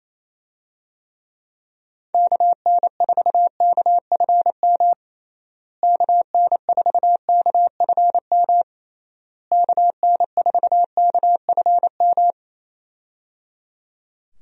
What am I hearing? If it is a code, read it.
Morse code, heard as a keyed tone, about 28 wpm, 700 Hz. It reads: KN4KFM KN4KFM KN4KFM
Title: USA Call Signs - 6 Characters - Repeated 3x - 28wpm